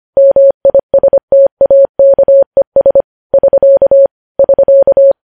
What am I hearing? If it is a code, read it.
Morse code, heard as a keyed tone, about 25 wpm, 550 Hz. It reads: MISTAKES <SK> <SK>
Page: news-headlines-in-morse-code-at-25-wpm